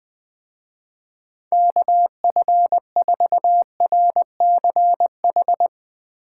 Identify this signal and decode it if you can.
Morse code, heard as a keyed tone, about 20 wpm, 700 Hz. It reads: KF4RCH
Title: Speed Racing - USA 2x3 Call Signs 17wpm